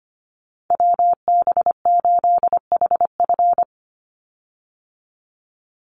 Morse code, heard as a keyed tone, about 25 wpm, 700 Hz. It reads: WB8HF
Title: Call Signs Repeated 3x - 25wpm